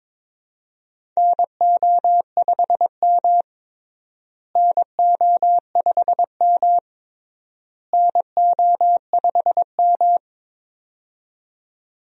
Morse code, heard as a keyed tone, about 22 wpm, 700 Hz. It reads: NO5M NO5M NO5M
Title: USA Call Signs - 4 Characters - Repeated 3x - 22wpm